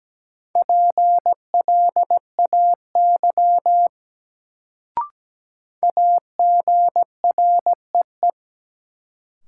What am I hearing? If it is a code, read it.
Morse code, heard as a keyed tone, about 17 wpm, 700 Hz. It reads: PLAY AGREE
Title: Sets of 5 Words Encore - New Word Drill - 17wpm